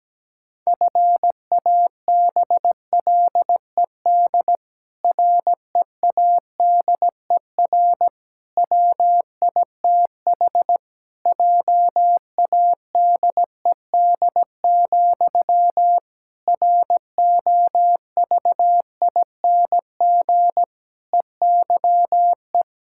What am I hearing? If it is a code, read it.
Morse code, heard as a keyed tone, about 17 wpm, 700 Hz. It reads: FABLED READER WITH JADED, ROVING EYE
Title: Pangram Challenge v2 - 17wpm